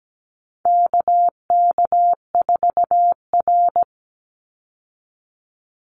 Morse code, heard as a keyed tone, about 17 wpm, 700 Hz. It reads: KK4R
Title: Call Signs - 17wpm